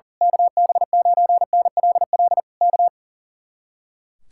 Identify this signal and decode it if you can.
Morse code, heard as a keyed tone, about 38 wpm, 700 Hz. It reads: KB9NLL K
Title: Speed-Racing Common QSO Phrases - 40wpm